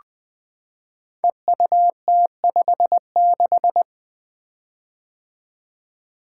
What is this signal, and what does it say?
Morse code, heard as a keyed tone, about 20 wpm, 700 Hz. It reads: EUT56
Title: Code Groups - 5 Letter:Numbers - 20wpm